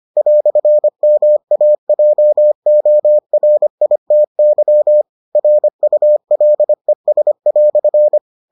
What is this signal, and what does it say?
Morse code, heard as a keyed tone, about 25 wpm, 600 Hz. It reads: "MAJORITY RULES"